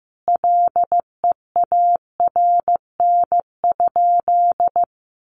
Morse code, heard as a keyed tone, about 15 wpm, 700 Hz. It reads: LEARN?
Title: Sentences from Top 300 Words 15wpm